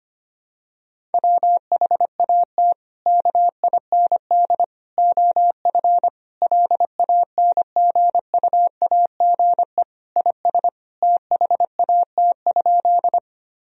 Morse code, heard as a keyed tone, about 25 wpm, 700 Hz. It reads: WHAT KIND OF LANGUAGE IS THAT?
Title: Sentences from Top 1000 Words - Review Part 3 - 25wpm